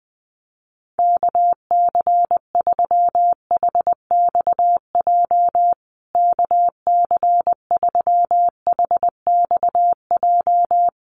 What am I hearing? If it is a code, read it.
Morse code, heard as a keyed tone, about 20 wpm, 700 Hz. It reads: KC3HXJ KC3HXJ
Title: USA Call Signs Repeated 3x 20wpm